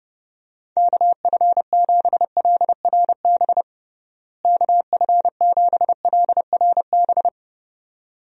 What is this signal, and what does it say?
Morse code, heard as a keyed tone, about 30 wpm, 700 Hz. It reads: KF7LRB KF7LRB
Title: USA Call Signs - 6 Characters - Repeated 2x - 30wpm